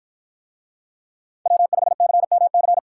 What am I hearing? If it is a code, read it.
Morse code, heard as a keyed tone, about 53 wpm, 700 Hz. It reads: W5CMC